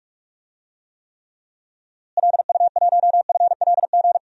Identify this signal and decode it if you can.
Morse code, heard as a keyed tone, about 46 wpm, 700 Hz. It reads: LU1FLG
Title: DXCC-100 Call Signs - 6 Characters - Repeated 3x - 45wpm